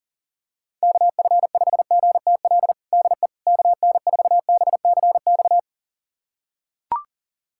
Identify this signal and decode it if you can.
Morse code, heard as a keyed tone, about 40 wpm, 700 Hz. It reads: KF5GTL DE KN4BCX
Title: USA Callsign DE Callsign 40wpm